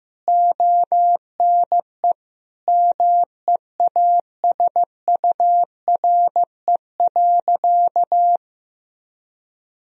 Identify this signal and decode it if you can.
Morse code, heard as a keyed tone, about 16 wpm, 700 Hz. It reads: ONE MEASURE.
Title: Sentences from Top 1000 Words - Review Part 3 - 15wpm